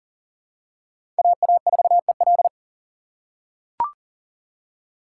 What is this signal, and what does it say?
Morse code, heard as a keyed tone, about 40 wpm, 700 Hz. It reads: AA4EL
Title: USA Call Signs - 5 Characters - 3x Character Spacing - 40wpm